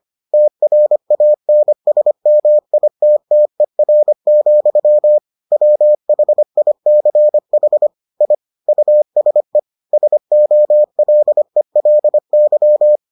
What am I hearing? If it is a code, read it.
Morse code, heard as a keyed tone, about 25 wpm, 600 Hz. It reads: TRANSMITTER, WHICH I USE SOLELY